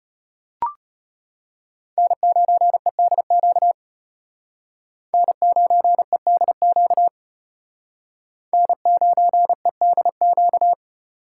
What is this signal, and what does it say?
Morse code, heard as a keyed tone, about 33 wpm, 700 Hz. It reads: N9EDQ N9EDQ N9EDQ